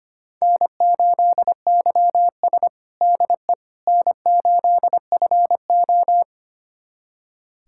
Morse code, heard as a keyed tone, about 25 wpm, 700 Hz. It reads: N8YS DE N8FO
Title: USA Callsign DE Callsign - 4 Characters - 25wpm